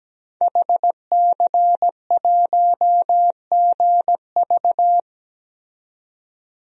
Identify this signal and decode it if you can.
Morse code, heard as a keyed tone, about 17 wpm, 700 Hz. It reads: HC1GV
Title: DXCC-100 Call Signs - 5 Characters - 17wpm